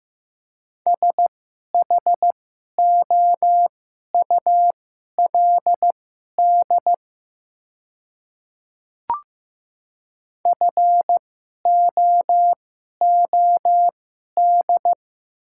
Morse code, heard as a keyed tone, about 15 wpm, spaced as for 11 wpm, 700 Hz. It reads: SHOULD FOOD
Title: 2x Character Spacing - Top 300 Words - 15wpm